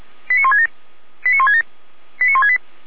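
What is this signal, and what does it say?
Five-tone selective call (ZVEI), heard as quick runs of five tones, tones of about 70 ms. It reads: 89167 89167 89168
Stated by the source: ZVEI five-tone selective call received by SDR.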